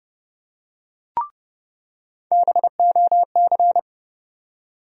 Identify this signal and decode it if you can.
Morse code, heard as a keyed tone, about 31 wpm, 700 Hz. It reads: BOC
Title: USA Call Sign Postfixes - 3 Letters 30wpm